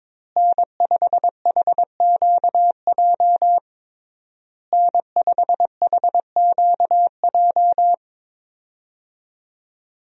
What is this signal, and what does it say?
Morse code, heard as a keyed tone, about 22 wpm, 700 Hz. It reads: N5HQJ N5HQJ
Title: USA Call Signs - 5 Characters - Repeated 3x - 22wpm